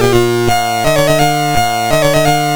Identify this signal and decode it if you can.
There is no code.